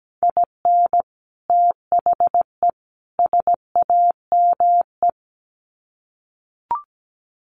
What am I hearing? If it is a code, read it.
Morse code, heard as a keyed tone, about 17 wpm, 700 Hz. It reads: IN THE SAME
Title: Sets of 3 Words 17wpm